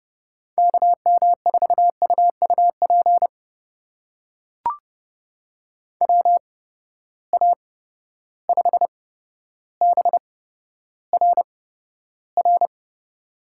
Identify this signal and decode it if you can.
Morse code, heard as a keyed tone, about 30 wpm, spaced as for 13 wpm, 700 Hz. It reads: KM4UUP WA5BRR
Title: USA Call Signs - 6 Characters - 8x Character Spacing - 30wpm